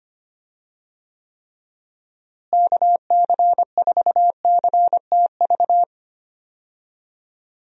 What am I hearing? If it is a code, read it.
Morse code, heard as a keyed tone, about 25 wpm, 700 Hz. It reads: KC4CTV
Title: USA Call Signs - 6 Characters - Repeated 2x - 25wpm